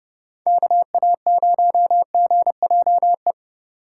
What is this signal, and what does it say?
Morse code, heard as a keyed tone, about 29 wpm, 700 Hz. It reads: KA0GJE